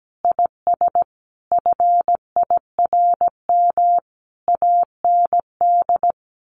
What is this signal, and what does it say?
Morse code, heard as a keyed tone, about 17 wpm, 700 Hz. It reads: IS FIRM AND
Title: Sentences from Top 1500 Words 17wpm